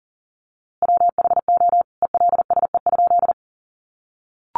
Morse code, heard as a keyed tone, about 40 wpm, 700 Hz. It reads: WHO ELSE?
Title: A Sound Of Thunder 40wpm